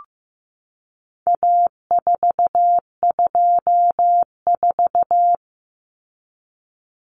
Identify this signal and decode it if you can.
Morse code, heard as a keyed tone, about 15 wpm, 700 Hz. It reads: A424
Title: Common Cut Numbers - 4-Digit 15wpm